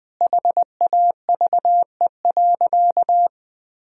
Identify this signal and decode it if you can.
Morse code, heard as a keyed tone, about 21 wpm, 700 Hz. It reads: HAVE.
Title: Sentences from Top 1000 Words - Review Part 2 - 20wpm